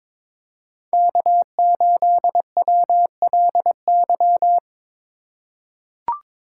Morse code, heard as a keyed tone, about 22 wpm, 700 Hz. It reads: K8WLY
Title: USA Call Signs - 5 Characters - 2x Character Spacing - 22wpm